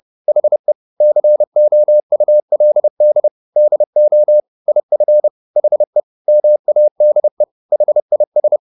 Morse code, heard as a keyed tone, about 30 wpm, 600 Hz. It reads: HE COULD DO IF HE MADE HIS
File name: the old man and the sea / the old man and the sea 1 30wpm